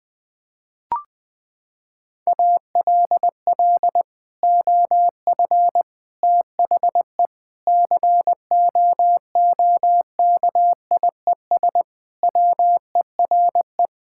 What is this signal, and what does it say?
Morse code, heard as a keyed tone, about 20 wpm, 700 Hz. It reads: ALL OF THE COOKIES WERE
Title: Sentences with 2-Word Phrases - 20wpm